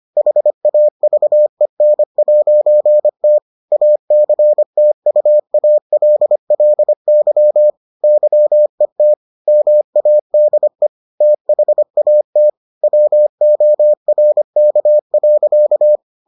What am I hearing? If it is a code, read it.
Morse code, heard as a keyed tone, about 25 wpm, 600 Hz. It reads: HAVEN'T ACTUALLY YET MADE THAT WORK.